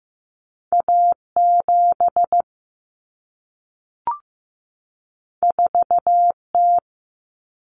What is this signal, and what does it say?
Morse code, heard as a keyed tone, about 15 wpm, 700 Hz. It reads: A7 4T
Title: Common Cut Numbers - 2-Digit 15wpm